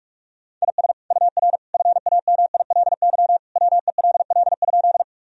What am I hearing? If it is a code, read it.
Morse code, heard as a keyed tone, about 46 wpm, 700 Hz. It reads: IS UR FAMILY WELL?
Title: Sentences from Top 400 Words and Abbreviations 45wpm